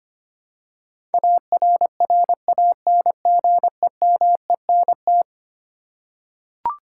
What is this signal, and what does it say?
Morse code, heard as a keyed tone, about 25 wpm, 700 Hz. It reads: ARRANGEMENT
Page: morse-code-ninja-practice-25wpm